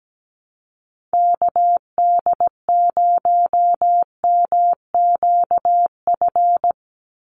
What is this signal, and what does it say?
Morse code, heard as a keyed tone, about 17 wpm, 700 Hz. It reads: KD0MQF
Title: USA Call Signs 17wpm